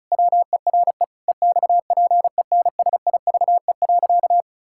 Morse code, heard as a keyed tone, about 34 wpm, 700 Hz. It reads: WERE EXPENSIVE.
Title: Pangram Challenge v2 - 35wpm